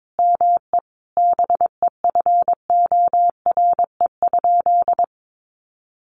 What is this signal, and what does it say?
Morse code, heard as a keyed tone, about 22 wpm, 700 Hz. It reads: ME BEFORE?
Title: Sentences from Top 900 Words 22wpm